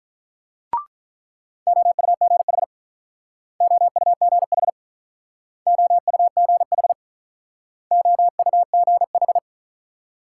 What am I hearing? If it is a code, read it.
Morse code, heard as a keyed tone, about 44 wpm, 700 Hz. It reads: OUGH OUGH OUGH OUGH